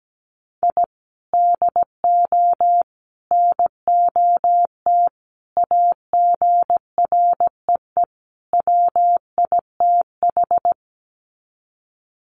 Morse code, heard as a keyed tone, about 17 wpm, 700 Hz. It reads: I DO NOT AGREE WITH
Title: Sets of 5 Words Encore 17wpm